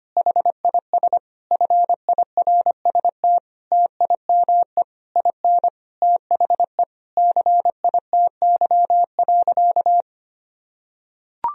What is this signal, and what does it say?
Morse code, heard as a keyed tone, about 25 wpm, 700 Hz. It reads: HIS FIRST TIME IN THE CITY.
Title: Sentences from Top 1000 Words - Review Part 4 - 25wpm